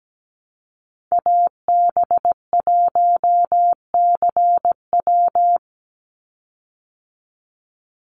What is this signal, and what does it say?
Morse code, heard as a keyed tone, about 17 wpm, 700 Hz. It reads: AB1CW
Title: Call Signs Repeated 3x - 17wpm